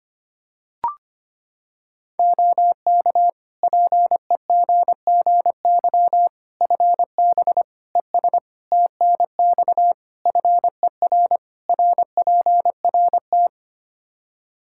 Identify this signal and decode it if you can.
Morse code, heard as a keyed tone, about 25 wpm, 700 Hz. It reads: OK PEGGY FB ES TNX FER RPRT
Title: Practice QSO - Exchange 2 - Info - 25wpm